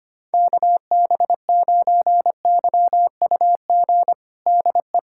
Morse code, heard as a keyed tone, about 25 wpm, 700 Hz. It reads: KB9YUG DE
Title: USA Callsign DE Callsign - 6 Characters - 25wpm